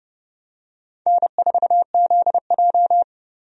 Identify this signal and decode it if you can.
Morse code, heard as a keyed tone, about 30 wpm, 700 Hz. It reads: N4ZJ
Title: USA Call Signs - 4 Characters - 30wpm